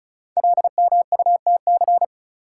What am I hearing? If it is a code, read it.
Morse code, heard as a keyed tone, about 36 wpm, 690 Hz. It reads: LMUTC
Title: Code Groups - 5 Letters - 35wpm